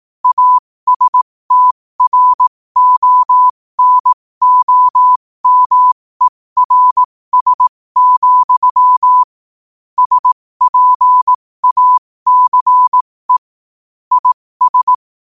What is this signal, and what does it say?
Morse code, heard as a keyed tone, about 18 wpm, spaced as for 15 wpm, 1000 Hz. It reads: ASTRONOMERS, SPACE IS
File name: qotdfiles / 15wpm